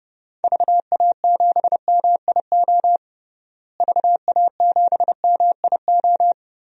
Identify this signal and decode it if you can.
Morse code, heard as a keyed tone, about 30 wpm, 700 Hz. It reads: VA7MIO VA7MIO
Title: DXCC-100 Call Signs - 6 Characters - Repeated 3x - 30wpm